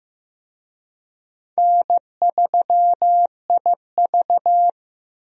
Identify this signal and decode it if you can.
Morse code, heard as a keyed tone, about 15 wpm, 700 Hz. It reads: N3IV
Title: USA Call Signs - 4 Characters - Repeated 2x - 15wpm